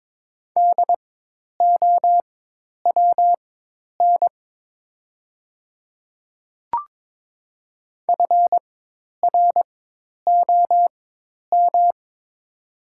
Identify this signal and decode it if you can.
Morse code, heard as a keyed tone, about 22 wpm, spaced as for 10 wpm, 700 Hz. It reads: DOWN FROM
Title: 4x Character Spacing - Top 100 Words - 22wpm